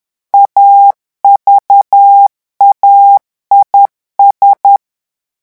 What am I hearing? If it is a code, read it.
Morse code, heard as a keyed tone, about 10 wpm, 800 Hz. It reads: AVAIS
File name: revision9 citations humoristiques vitesse 10 mots minute